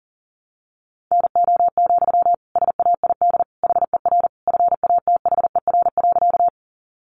Morse code, heard as a keyed tone, about 40 wpm, 700 Hz. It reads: NO, SAID HER FATHER.
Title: Sentences from Top 300 Words 40wpm